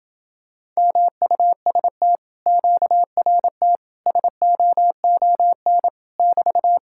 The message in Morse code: MUST QRT SOON <BT>

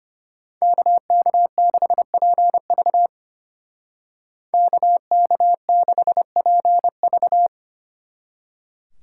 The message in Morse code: KK6PV KK6PV

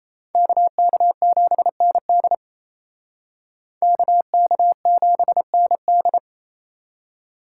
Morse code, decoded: KK7ND KK7ND

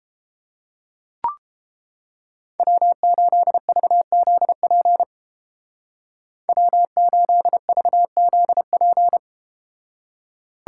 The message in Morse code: W8VZP W8VZP

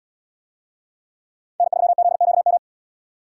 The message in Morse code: N3ZQM

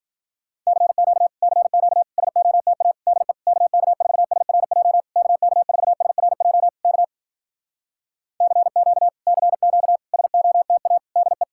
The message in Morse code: CQ CQ SOTA DE KK4SRJ KK4SRJ K CQ CQ SOTA DE

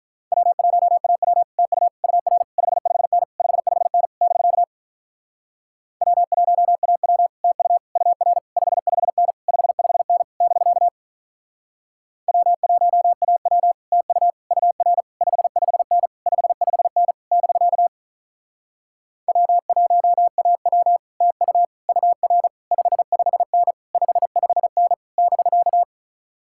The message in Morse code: W1AW TU UR 55N 55N <BK> W1AW TU UR 55N 55N <BK> W1AW TU UR 55N 55N <BK> W1AW TU UR 55N 55N <BK>

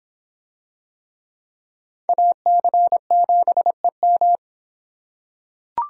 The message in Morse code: AC7EM

